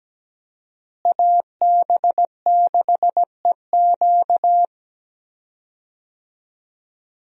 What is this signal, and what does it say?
Morse code, heard as a keyed tone, about 17 wpm, 700 Hz. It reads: AB6EQ
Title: USA Call Signs - 5 Characters - Repeated 3x - 17wpm